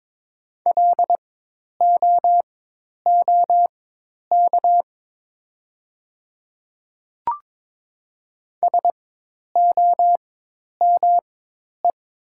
Morse code, heard as a keyed tone, about 22 wpm, spaced as for 10 wpm, 700 Hz. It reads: LOOK SOME
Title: 4x Character Spacing - Top 100 Words - 22wpm